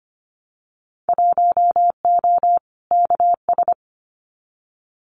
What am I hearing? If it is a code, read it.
Morse code, heard as a keyed tone, about 25 wpm, 700 Hz. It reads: 1O KS
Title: Winter Field Day Exchanges 25wpm